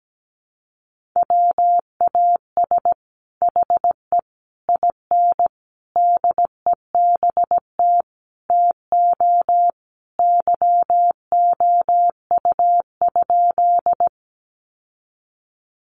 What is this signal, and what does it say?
Morse code, heard as a keyed tone, about 17 wpm, 700 Hz. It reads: WAS HE IN DEBT TO YOU?
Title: Sentences from Top 1300 Words 17wpm